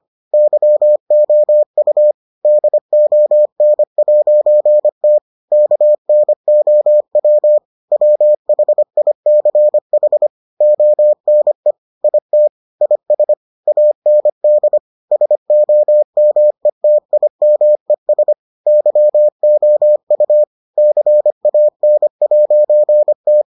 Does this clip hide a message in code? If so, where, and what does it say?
Yes: Morse code from 0.3 to 23.4 s, heard as a keyed tone, about 25 wpm, 600 Hz. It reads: YOU DON'T KNOW WHICH ONE IT IS AND SOMETIMES YOU CAN'T